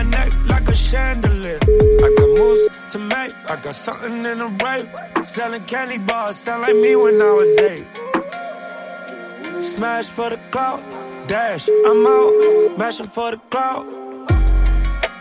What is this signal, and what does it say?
Ringback tone (Europe, 425 Hz), with speech.